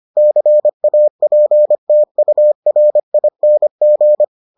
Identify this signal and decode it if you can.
Morse code, heard as a keyed tone, about 25 wpm, 600 Hz. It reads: CAPTURING